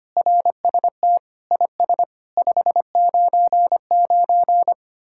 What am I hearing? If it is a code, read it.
Morse code, heard as a keyed tone, about 31 wpm, 700 Hz. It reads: RST IS 599
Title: Speed-Racing Common QSO Phrases Encore - 25wpm